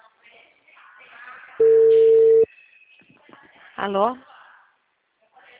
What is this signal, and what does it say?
Ringback tone (China, 450 Hz), with speech.